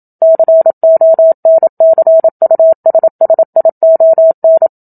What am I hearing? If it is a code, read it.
Morse code, heard as a keyed tone, about 27 wpm, 650 Hz. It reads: CONCUSSION
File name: Book Two / 01-Chapter 1 - 27 WPM